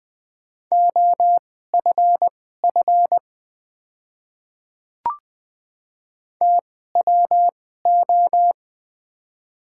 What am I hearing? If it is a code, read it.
Morse code, heard as a keyed tone, about 20 wpm, spaced as for 14 wpm, 700 Hz. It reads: OFF TWO
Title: 2x Character Spacing - Top 200 Words - 20wpm